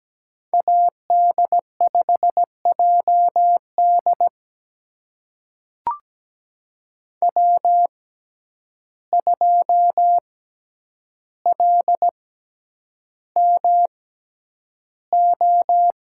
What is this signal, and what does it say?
Morse code, heard as a keyed tone, about 17 wpm, spaced as for 9 wpm, 700 Hz. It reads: AD5JD W2LMO
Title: USA Call Signs - 5 Characters - 6x Character Spacing - 17wpm